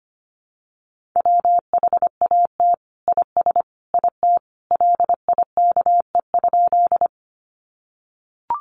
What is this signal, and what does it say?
Morse code, heard as a keyed tone, about 25 wpm, 700 Hz. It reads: WHAT IS IT LIKE?
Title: Sentences from Top 100 Words 25wpm